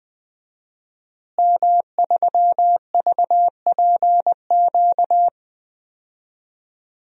Morse code, heard as a keyed tone, about 21 wpm, 700 Hz. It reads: M3VPQ
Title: DXCC-100 Call Signs - 5 Characters - Repeated 3x - 20wpm